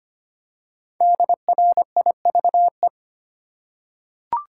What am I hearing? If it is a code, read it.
Morse code, heard as a keyed tone, about 25 wpm, 700 Hz. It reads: DRIVE
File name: 5 Letter Words - First 3 Letters Given - 25wpm